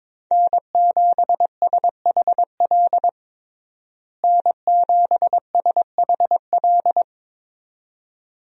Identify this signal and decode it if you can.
Morse code, heard as a keyed tone, about 22 wpm, 700 Hz. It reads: N7SHL N7SHL
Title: USA Call Signs - 5 Characters - Repeated 2x - 22wpm